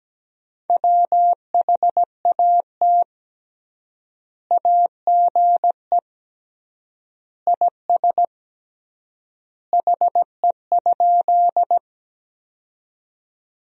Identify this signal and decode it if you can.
Morse code, heard as a keyed tone, about 17 wpm, 700 Hz. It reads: WHAT AGE IS HE?
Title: 3x Word-Spacing - Sentences from Top 500 Words - 17wpm